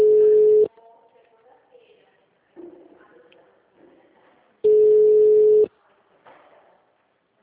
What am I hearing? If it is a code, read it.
Ringback tone (Europe, 425 Hz), with speech.